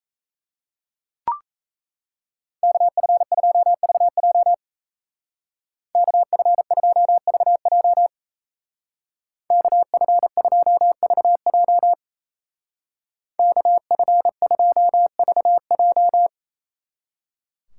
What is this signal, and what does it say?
Morse code, heard as a keyed tone, about 34 wpm, 700 Hz. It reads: KF2VJ KF2VJ KF2VJ KF2VJ
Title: Speed Racing - USA 2x2 Call Signs 28wpm